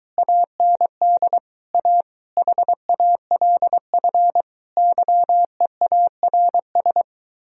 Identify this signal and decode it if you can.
Morse code, heard as a keyed tone, about 23 wpm, 700 Hz. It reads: AND A HALF YEARS